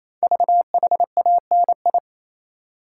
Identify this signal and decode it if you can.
Morse code, heard as a keyed tone, about 27 wpm, 700 Hz. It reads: VHANI
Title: Code Groups - 5 Letters - 28wpm